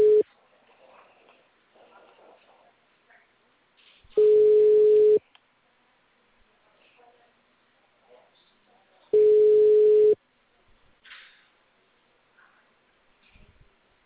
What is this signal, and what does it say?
Ringback tone (Europe, 425 Hz).
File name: out- -2016- - - .82595